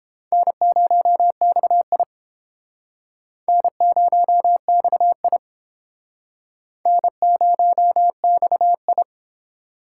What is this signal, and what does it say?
Morse code, heard as a keyed tone, about 31 wpm, 700 Hz. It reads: N0XI N0XI N0XI